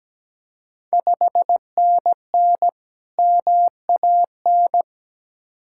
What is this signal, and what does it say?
Morse code, heard as a keyed tone, about 17 wpm, 700 Hz. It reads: 5NN MAN
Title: Ontario QSO Party Exchange 17wpm